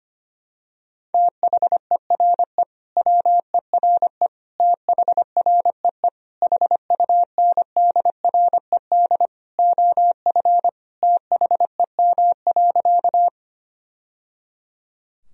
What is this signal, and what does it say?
Morse code, heard as a keyed tone, about 25 wpm, 700 Hz. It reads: THERE WERE THREE HUNDRED OF THEM.
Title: Sentences from Top 1000 Words - Review Part 2 - 25wpm